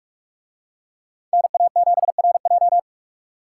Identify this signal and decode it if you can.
Morse code, heard as a keyed tone, about 44 wpm, 700 Hz. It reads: NA7RJ